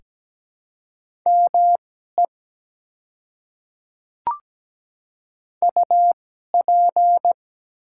Morse code, heard as a keyed tone, about 17 wpm, spaced as for 12 wpm, 700 Hz. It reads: ME UP
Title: 2x Character Spacing - Top 200 Words - 17wpm